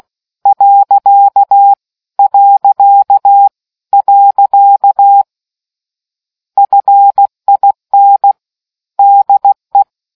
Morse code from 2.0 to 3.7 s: .

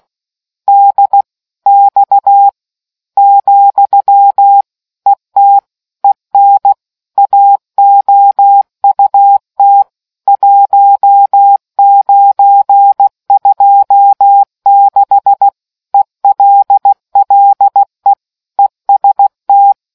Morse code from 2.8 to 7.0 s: , ET EN

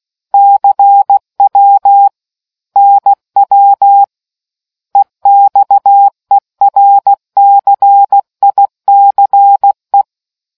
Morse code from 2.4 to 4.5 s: NW

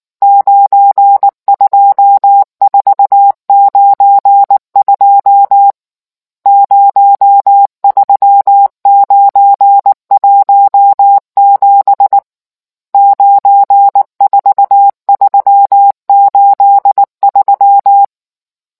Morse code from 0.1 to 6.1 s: 92492

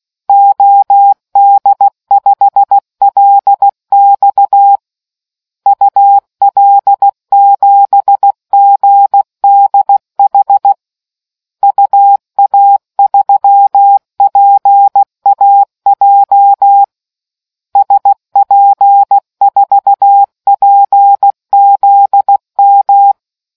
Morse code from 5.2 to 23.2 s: UL7GDH UA3PAJ SP4PZM

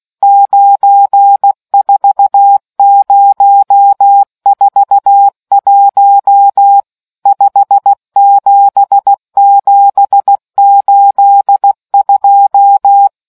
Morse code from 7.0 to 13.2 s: 57782

